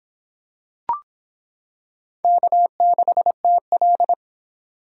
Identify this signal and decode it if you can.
Morse code, heard as a keyed tone, about 25 wpm, 700 Hz. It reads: K6TL